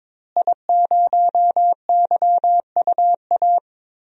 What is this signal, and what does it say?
Morse code, heard as a keyed tone, about 22 wpm, 700 Hz. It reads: I0YUA
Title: DXCC-100 Call Signs - 5 Characters - Repeated 3x - 22wpm